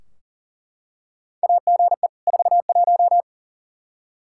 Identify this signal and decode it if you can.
Morse code, heard as a keyed tone, about 40 wpm, 700 Hz. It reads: AGE 41
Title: Practice QSO - Exchange 3 - More Info - 40wpm